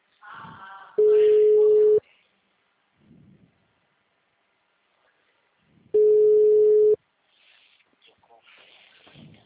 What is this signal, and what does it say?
Ringback tone (Europe, 425 Hz), with speech.
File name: out- -2003- - - .477